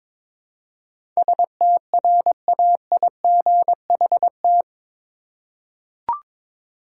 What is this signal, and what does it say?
Morse code, heard as a keyed tone, about 21 wpm, 700 Hz. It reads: STRAIGHT
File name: practice / 22wpm / Eight Letter Words - 22wpm